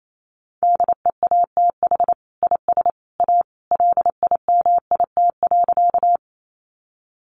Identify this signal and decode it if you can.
Morse code, heard as a keyed tone, about 28 wpm, 700 Hz. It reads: DEATH IS A LIMIT.
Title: Sentences from Top 1500 Words 28wpm